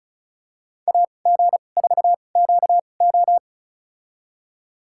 Morse code, heard as a keyed tone, about 36 wpm, spaced as for 25 wpm, 700 Hz. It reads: AG4QO
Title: USA Call Signs - 2x Character Spacing - 35wpm